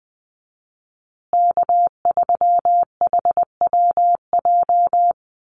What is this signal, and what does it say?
Morse code, heard as a keyed tone, about 20 wpm, 700 Hz. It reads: K3HWJ
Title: USA Call Signs Repeated 3x 20wpm